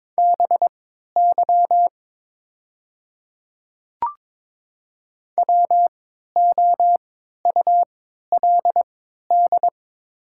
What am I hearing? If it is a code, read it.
Morse code, heard as a keyed tone, about 23 wpm, spaced as for 13 wpm, 700 Hz. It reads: BY WOULD
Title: 3x Character Spacing - Top 100 Words - 22wpm